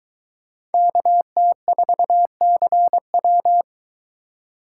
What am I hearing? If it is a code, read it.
Morse code, heard as a keyed tone, about 23 wpm, 700 Hz. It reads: KT4CW